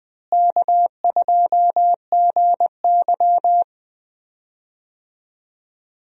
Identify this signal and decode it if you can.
Morse code, heard as a keyed tone, about 21 wpm, 700 Hz. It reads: K2GY
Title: USA Call Signs - 4 Characters - Repeated 2x - 20wpm